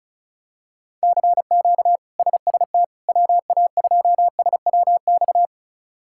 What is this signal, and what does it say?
Morse code, heard as a keyed tone, about 36 wpm, 700 Hz. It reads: CQ SST WA2SWX
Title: K1USN SST QSO Practice - 35wpm